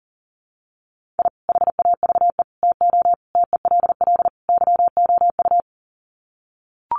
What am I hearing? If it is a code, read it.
Morse code, heard as a keyed tone, about 40 wpm, 700 Hz. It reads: I HAVE TO TELL YOU
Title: Sets of 5 Words 40wpm